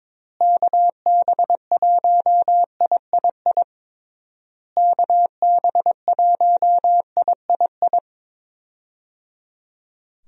Morse code, heard as a keyed tone, about 22 wpm, 700 Hz. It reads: KB1III KB1III
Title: USA Call Signs - 6 Characters - Repeated 3x - 22wpm